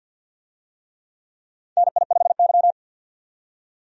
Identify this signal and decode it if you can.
Morse code, heard as a keyed tone, about 47 wpm, 700 Hz. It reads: NI5Y
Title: USA Call Signs - 4 Characters - 50wpm